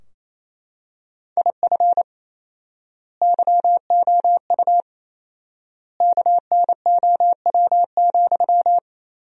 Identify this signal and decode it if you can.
Morse code, heard as a keyed tone, about 28 wpm, 700 Hz. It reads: IF YOU KNOW,
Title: 4x Word-Spacing - Sentences from Top 300 Words - 28wpm